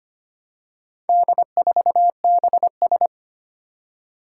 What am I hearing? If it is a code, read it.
Morse code, heard as a keyed tone, about 25 wpm, 700 Hz. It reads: D4BS
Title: Four Challenging Letter-Number Combinations 25wpm